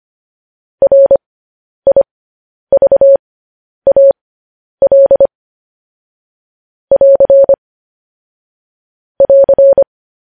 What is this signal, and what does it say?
Morse code, heard as a keyed tone, about 25 wpm, spaced as for 10 wpm, 550 Hz. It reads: RIVAL <AR> <AR>
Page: news-headlines-in-morse-code-at-10-wpm